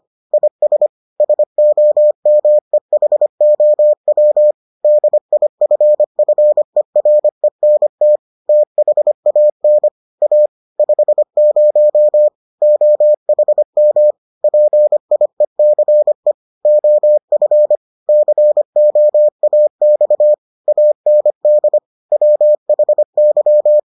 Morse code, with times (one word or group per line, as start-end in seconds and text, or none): 0.3-0.9 IS
1.2-4.5 SOMEHOW
4.8-8.2 DIFFERENT
8.5-9.9 THAN
10.2-10.5 A
10.8-12.3 50
12.6-14.1 OHM
14.4-16.3 PIECE
16.7-17.8 OF
18.1-20.3 COAX
20.7-21.8 AND
22.1-23.8 WHY